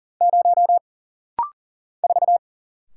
Morse code, heard as a keyed tone, about 40 wpm, 700 Hz. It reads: 0 4
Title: Single Letter-Number Alt - Rapid-Fire 40wpm